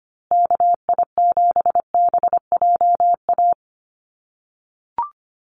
Morse code, heard as a keyed tone, about 25 wpm, 700 Hz. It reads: KI7BJA
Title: USA Call Signs Repeated 3x 25wpm